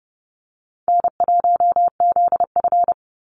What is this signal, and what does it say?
Morse code, heard as a keyed tone, about 30 wpm, 700 Hz. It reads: N1ZF
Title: NA SOTA Chaser Callsigns 30wpm